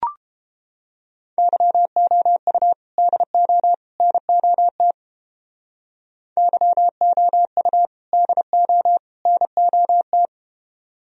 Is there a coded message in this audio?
Yes: Morse code from 1.4 to 10.3 s, heard as a keyed tone, about 33 wpm, 700 Hz. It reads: YOU DO NOT YOU DO NOT